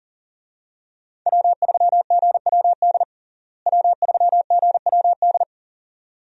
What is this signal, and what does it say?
Morse code, heard as a keyed tone, about 40 wpm, 700 Hz. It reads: W3GWD W3GWD
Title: USA Call Signs - 5 Characters - Repeated 2x - 40wpm